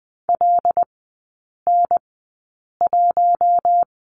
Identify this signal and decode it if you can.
Morse code, heard as a keyed tone, about 20 wpm, spaced as for 8 wpm, 700 Hz. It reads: LN1
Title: 121 Lesson 20e - Number 3 - Three Characters 20wpm